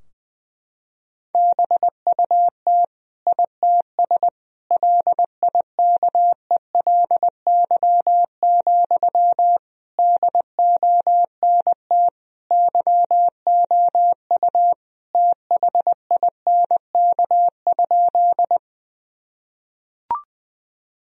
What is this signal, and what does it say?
Morse code, heard as a keyed tone, about 21 wpm, 700 Hz. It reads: BUT ITS LIKELY, DONT YOU THINK?
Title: Sentences from Top 1000 Words - Review Part 4 - 20wpm